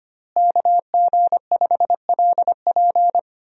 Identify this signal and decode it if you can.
Morse code, heard as a keyed tone, about 25 wpm, 700 Hz. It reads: KG5LP
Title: USA Call Signs - 5 Characters - Repeated 3x - 25wpm